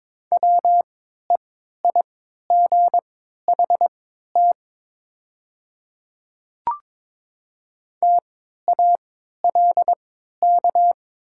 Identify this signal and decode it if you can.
Morse code, heard as a keyed tone, about 22 wpm, spaced as for 12 wpm, 700 Hz. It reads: WEIGHT TALK
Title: 3x Character Spacing - Top 500 Words - 22wpm